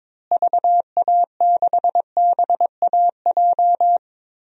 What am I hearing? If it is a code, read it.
Morse code, heard as a keyed tone, about 22 wpm, 700 Hz. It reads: VA6BAJ
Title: DXCC-100 Call Signs - 6 Characters - Repeated 3x - 22wpm